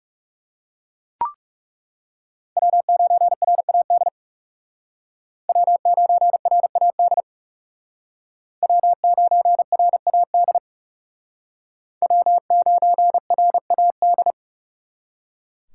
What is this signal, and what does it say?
Morse code, heard as a keyed tone, about 40 wpm, 700 Hz. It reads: W9RAD W9RAD W9RAD W9RAD